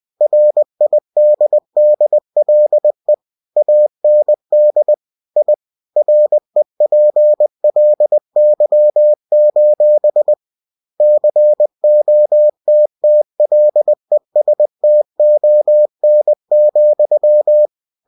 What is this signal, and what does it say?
Morse code, heard as a keyed tone, about 20 wpm, 600 Hz. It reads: RIDDLE AND I REPLY: COTTLESTON,
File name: cw-books / winnie-the-pooh / winnie-20wpm-page-0045